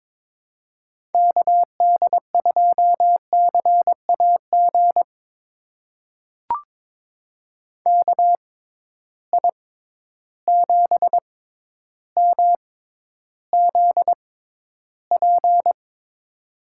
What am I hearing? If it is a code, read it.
Morse code, heard as a keyed tone, about 22 wpm, spaced as for 11 wpm, 700 Hz. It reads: KD2CAG KI7MZP
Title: USA Call Signs - 6 Characters - 6x Character Spacing - 22wpm